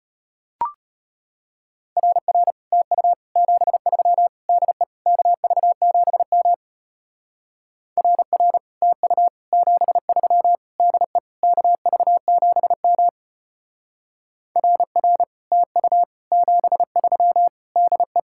Morse code, heard as a keyed tone, about 31 wpm, 700 Hz. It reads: RR TU 73 DE KV7M RR TU 73 DE KV7M RR TU 73 DE
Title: Speed-Racing Common QSO Phrases - 25wpm